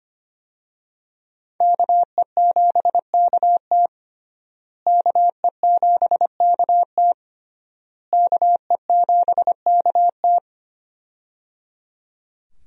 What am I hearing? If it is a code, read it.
Morse code, heard as a keyed tone, about 25 wpm, 700 Hz. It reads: KE7KT KE7KT KE7KT
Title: USA Call Signs - 5 Characters - Repeated 3x - 25wpm